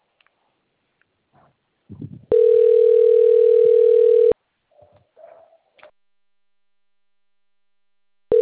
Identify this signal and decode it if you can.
Ringback tone (North America), 440 + 480 Hz.